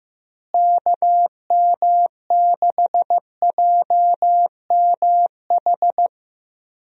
Morse code, heard as a keyed tone, about 15 wpm, 700 Hz. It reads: KM6JMH
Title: USA Call Signs - 6 Characters - Repeated 3x - 15wpm